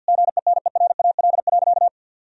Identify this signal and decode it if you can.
Morse code, heard as a keyed tone, about 52 wpm, 700 Hz. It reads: GENERAL.
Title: Sentences from Top 1000 Words - Review Part 3 - 50wpm